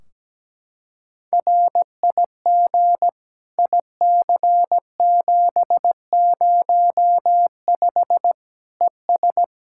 Morse code, heard as a keyed tone, about 17 wpm, 700 Hz. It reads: RIG IC705 ES
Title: Practice QSO - Exchange 2 - Info - 17wpm